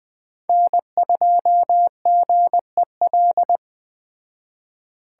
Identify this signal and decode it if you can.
Morse code, heard as a keyed tone, about 19 wpm, 700 Hz. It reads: N2GEL